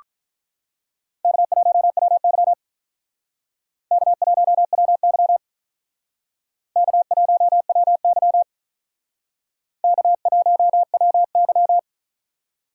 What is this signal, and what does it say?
Morse code, heard as a keyed tone, about 42 wpm, 700 Hz. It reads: K1WY K1WY K1WY K1WY
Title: Speed Racing - USA 1x2 Call Signs 35wpm